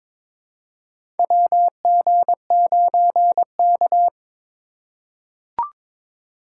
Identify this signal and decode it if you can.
Morse code, heard as a keyed tone, about 22 wpm, 700 Hz. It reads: WG9K
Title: USA Call Signs - 4 Characters - 2x Character Spacing - 22wpm